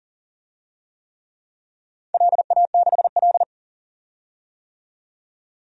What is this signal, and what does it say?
Morse code, heard as a keyed tone, about 40 wpm, 700 Hz. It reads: LA6L
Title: DXCC-100 Call Signs - 4 Characters - 40wpm